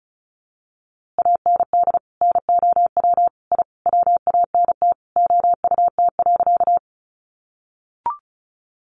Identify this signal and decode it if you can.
Morse code, heard as a keyed tone, about 34 wpm, 700 Hz. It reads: AND NOW I WANT OUT.
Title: A Sound Of Thunder 35wpm